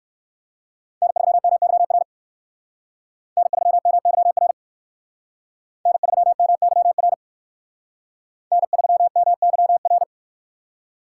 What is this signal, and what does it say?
Morse code, heard as a keyed tone, about 57 wpm, 700 Hz. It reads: N3MYR N3MYR N3MYR N3MYR